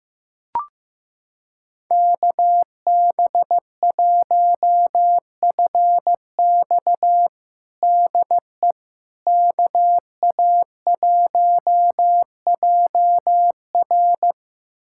Morse code, heard as a keyed tone, about 15 wpm, 700 Hz. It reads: KB1FX DE KA1JR
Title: USA Callsign DE Callsign - 5 Characters - 15wpm